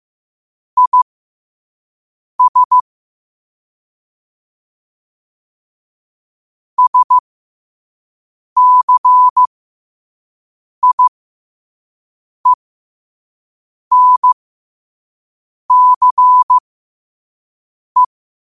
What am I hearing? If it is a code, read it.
Morse code, heard as a keyed tone, about 15 wpm, spaced as for 5 wpm, 1000 Hz. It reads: IS SCIENCE
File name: qotdfiles / 5wpm